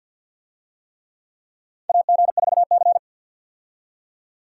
Morse code, heard as a keyed tone, about 53 wpm, 700 Hz. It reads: AG4C
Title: USA Call Signs - 4 Characters - 50wpm